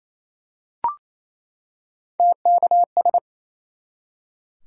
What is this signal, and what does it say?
Morse code, heard as a keyed tone, about 28 wpm, 700 Hz. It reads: TKS
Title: Top 90 QSO Elements 28wpm